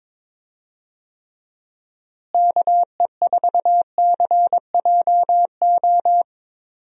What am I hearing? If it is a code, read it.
Morse code, heard as a keyed tone, about 21 wpm, 700 Hz. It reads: KE4CJO
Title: USA Call Signs - 6 Characters - Repeated 2x - 22wpm